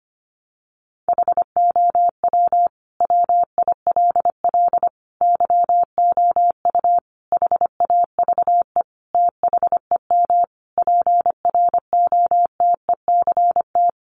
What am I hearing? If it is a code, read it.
Morse code, heard as a keyed tone, about 25 wpm, 700 Hz. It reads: HOW WILL YOU HAVE THEM PROTECT
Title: Sentences from Top 900 Words 25wpm